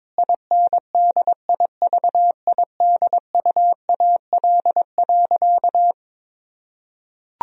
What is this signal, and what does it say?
Morse code, heard as a keyed tone, about 23 wpm, 700 Hz. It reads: INDIVIDUAL.
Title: Sentences from Top 1000 Words - Review Part 4 - 22wpm